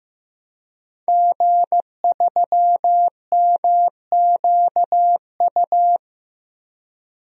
Morse code, heard as a keyed tone, about 15 wpm, 700 Hz. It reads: G3MQU
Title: DXCC-100 Call Signs - 5 Characters - 15wpm